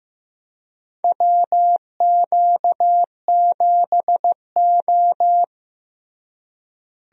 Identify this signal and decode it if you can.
Morse code, heard as a keyed tone, about 15 wpm, 700 Hz. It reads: WQ7O
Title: USA Call Signs - 4 Characters - Repeated 2x - 15wpm